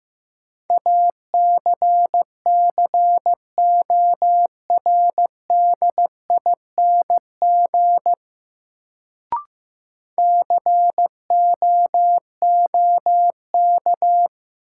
Morse code, heard as a keyed tone, about 15 wpm, 700 Hz. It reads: ACCORDING COOK
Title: Sentences from Top 700 Words - New Word Drill - 15wpm